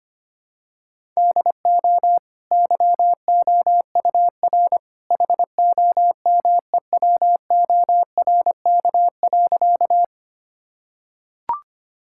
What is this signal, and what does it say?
Morse code, heard as a keyed tone, about 25 wpm, 700 Hz. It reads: DO YOUR HOMEWORK.
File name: Easy Sentences with Hints - 25wpm